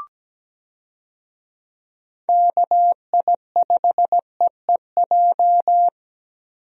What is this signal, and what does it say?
Morse code, heard as a keyed tone, about 17 wpm, 700 Hz. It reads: KI5EEJ
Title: USA Call Signs - 6 Characters - Repeated 3x - 17wpm